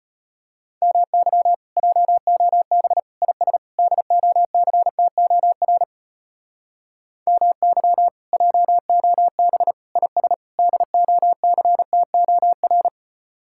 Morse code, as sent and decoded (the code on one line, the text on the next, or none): -- -.-- / .--- --- -... / .. ... / -.. --- -.-. - --- .-. / -- -.-- / .--- --- -... / .. ... / -.. --- -.-. - --- .-.
MY JOB IS DOCTOR MY JOB IS DOCTOR